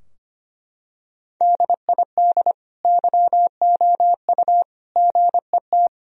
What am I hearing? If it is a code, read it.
Morse code, heard as a keyed tone, about 25 wpm, 700 Hz. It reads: DID YOU GET
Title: Sentences from Top 1000 Words - Review Part 3 - 25wpm